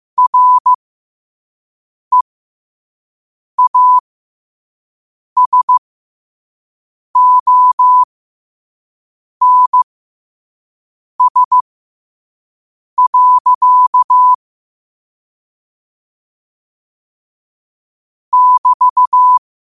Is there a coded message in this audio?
Yes: Morse code from 0.2 to 19.4 s, heard as a keyed tone, about 15 wpm, spaced as for 5 wpm, 1000 Hz. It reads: REASONS. <BT>